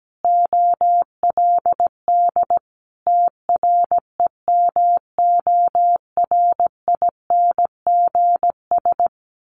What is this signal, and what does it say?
Morse code, heard as a keyed tone, about 17 wpm, 700 Hz. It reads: OLD TREMORINGS
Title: A Sound Of Thunder 17wpm